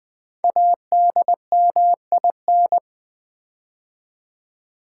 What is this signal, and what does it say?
Morse code, heard as a keyed tone, about 20 wpm, 700 Hz. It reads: ADMIN
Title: Build A Word - 5 Letters At A Time - 10-Letter Words - 20wpm